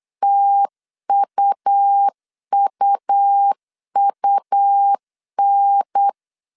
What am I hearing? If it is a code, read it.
Morse code, heard as a keyed tone, about 8 wpm, 800 Hz. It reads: TUUUN
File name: CW / DET / 1 3 1500C S B 8wpm 4141